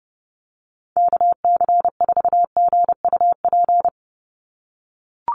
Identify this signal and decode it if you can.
Morse code, heard as a keyed tone, about 28 wpm, 700 Hz. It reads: KC4GUP
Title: USA Call Signs 30wpm